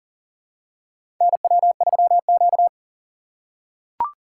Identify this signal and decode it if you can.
Morse code, heard as a keyed tone, about 40 wpm, 700 Hz. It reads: NW3Q
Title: USA Call Signs - 4 Characters - 2x Character Spacing - 40wpm